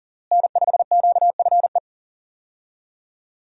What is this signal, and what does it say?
Morse code, heard as a keyed tone, about 37 wpm, 700 Hz. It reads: N5QFE
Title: USA Call Signs - 5 Characters - Repeated 3x - 40wpm